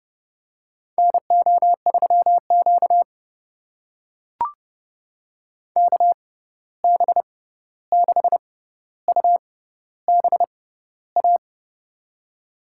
Morse code, heard as a keyed tone, about 30 wpm, spaced as for 14 wpm, 700 Hz. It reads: NO3Q KB6UBA
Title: USA Call Signs - 6x Character Spacing - 30wpm